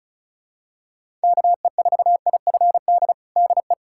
Morse code, heard as a keyed tone, about 36 wpm, 700 Hz. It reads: KE4IFD DE
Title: USA Callsign DE Callsign - 6 Characters - 35wpm